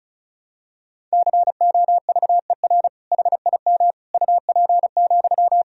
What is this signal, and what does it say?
Morse code, heard as a keyed tone, about 36 wpm, 700 Hz. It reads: COVER HIM UP,
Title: Sentences from Top 1000 Words - Review Part 1 - 35wpm